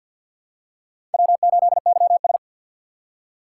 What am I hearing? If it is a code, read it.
Morse code, heard as a keyed tone, about 48 wpm, 700 Hz. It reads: W8YS
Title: USA Call Signs - 4 Characters - 6x Character Spacing - 50wpm